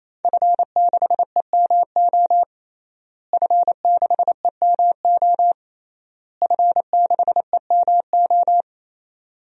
Morse code, heard as a keyed tone, about 28 wpm, 700 Hz. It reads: F6EMO F6EMO F6EMO
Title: DXCC-100 Call Signs - 5 Characters - Repeated 3x - 28wpm